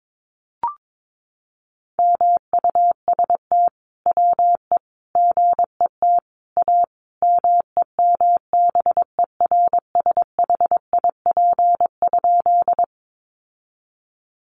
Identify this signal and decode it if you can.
Morse code, heard as a keyed tone, about 22 wpm, 700 Hz. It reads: MUST WE GET A MEMBERSHIP?
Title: Sentences from Top 1400 Words 22wpm